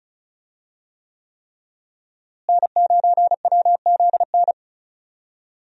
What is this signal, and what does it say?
Morse code, heard as a keyed tone, about 34 wpm, 700 Hz. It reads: N9WZN
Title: USA Call Signs - 5 Characters - Repeated 3x - 35wpm